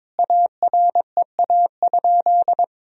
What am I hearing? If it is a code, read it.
Morse code, heard as a keyed tone, about 22 wpm, 700 Hz. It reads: AREA?
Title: Sentences from Top 1000 Words - Review Part 3 - 22wpm